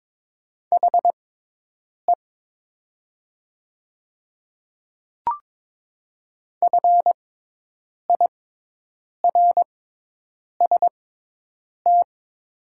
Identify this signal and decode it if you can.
Morse code, heard as a keyed tone, about 22 wpm, spaced as for 8 wpm, 700 Hz. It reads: HE FIRST
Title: 6x Character Spacing - Top 300 Words - 22wpm